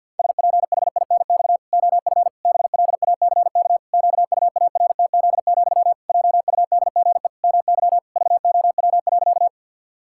Morse code, heard as a keyed tone, about 50 wpm, 700 Hz. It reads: SPHINX OF BLACK QUARTZ, JUDGE MY VOW.
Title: Pangram Challenge v2 - 50wpm